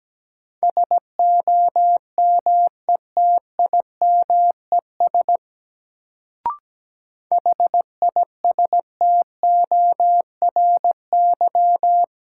Morse code, heard as a keyed tone, about 17 wpm, 700 Hz. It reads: SOMETIMES HISTORY
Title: Sentences from Top 600 Words - New Word Drill - 17wpm